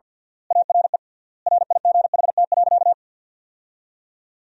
Morse code, heard as a keyed tone, about 53 wpm, 700 Hz. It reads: ARE RIGHT.
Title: 3x Word-Spacing - Sentences from Top 500 Words - 50wpm